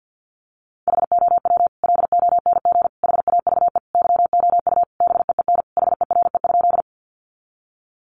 Morse code, heard as a keyed tone, about 50 wpm, 700 Hz. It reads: HOW LONG HAVE YOU BEEN HERE?
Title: Sentences from Top 200 Words 50wpm